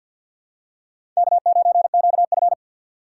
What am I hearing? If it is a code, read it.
Morse code, heard as a keyed tone, about 52 wpm, 700 Hz. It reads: K9QF
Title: USA Call Signs - 2x Character Spacing - 50wpm